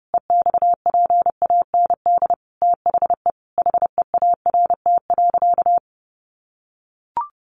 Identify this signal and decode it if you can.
Morse code, heard as a keyed tone, about 30 wpm, 700 Hz. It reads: EXPAND THE HEART.
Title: Sentences from Top 1400 Words 30wpm